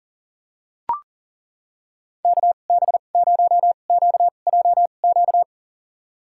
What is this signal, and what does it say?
Morse code, heard as a keyed tone, about 40 wpm, spaced as for 29 wpm, 700 Hz. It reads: KB0QJQ
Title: USA Call Signs - 6 Characters - 2x Character Spacing - 40wpm